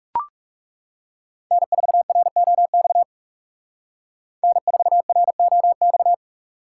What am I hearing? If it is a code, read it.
Morse code, heard as a keyed tone, about 38 wpm, 700 Hz. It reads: N4ROX N4ROX